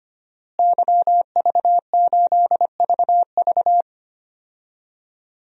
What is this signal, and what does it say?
Morse code, heard as a keyed tone, about 25 wpm, 700 Hz. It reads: YV8VV
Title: DXCC-100 Call Signs - 5 Characters - Repeated 2x - 25wpm